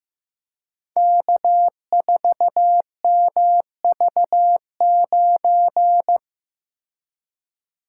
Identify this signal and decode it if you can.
Morse code, heard as a keyed tone, about 15 wpm, 700 Hz. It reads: K4MV9
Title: Code Groups - 5 Letter:Numbers - 15wpm